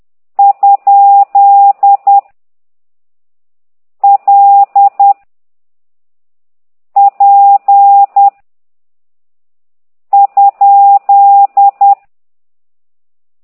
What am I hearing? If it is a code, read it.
Morse code, heard as a keyed tone, about 10 wpm, 800 Hz. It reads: ? L P ?
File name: ct2idw / exerc8